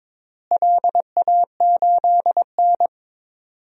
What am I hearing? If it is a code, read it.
Morse code, heard as a keyed tone, about 21 wpm, 700 Hz. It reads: LA8N
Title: DXCC-100 Call Signs - 4 Characters - Repeated 3x - 22wpm